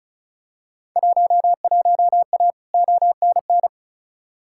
Morse code, heard as a keyed tone, about 36 wpm, 700 Hz. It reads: 11A ONN
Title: Field Day Exchanges Encore - 35wpm